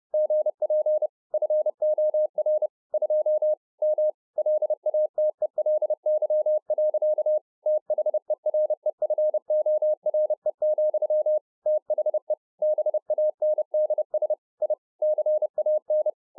Morse code, heard as a keyed tone, about 30 wpm, 610 Hz. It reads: GP FOR 2 M LATELY. THEREFORE, THE BANDS I CAN